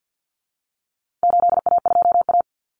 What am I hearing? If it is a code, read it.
Morse code, heard as a keyed tone, about 50 wpm, 700 Hz. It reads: 8A2A
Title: Common Cut Numbers - 4-Digit 50wpm